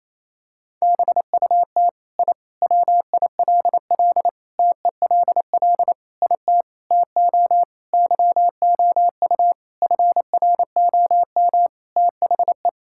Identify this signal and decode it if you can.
Morse code, heard as a keyed tone, about 28 wpm, 700 Hz. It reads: BUT I WILL TELL IT TO YOU FROM THE
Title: Sentences from Top 1000 Words - Review Part 5 - 28wpm